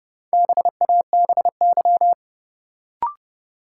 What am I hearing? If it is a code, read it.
Morse code, heard as a keyed tone, about 31 wpm, 700 Hz. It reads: BABY
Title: Sentences from Top 1300 Words - New Word Drill Alt - 30wpm